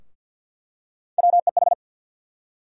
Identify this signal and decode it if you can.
Morse code, heard as a keyed tone, about 51 wpm, 700 Hz. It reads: WEH
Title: USA Call Sign Postfixes - 3 Letters 50wpm